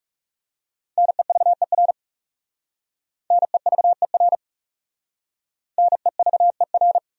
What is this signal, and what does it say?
Morse code, heard as a keyed tone, about 35 wpm, 700 Hz. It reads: NEVER NEVER NEVER
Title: Speed-Racing Top 300 Words 30wpm